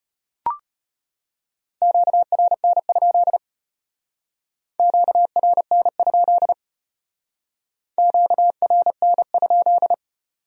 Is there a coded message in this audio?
Yes: Morse code from 1.8 to 9.9 s, heard as a keyed tone, about 33 wpm, 700 Hz. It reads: QRN? QRN? QRN?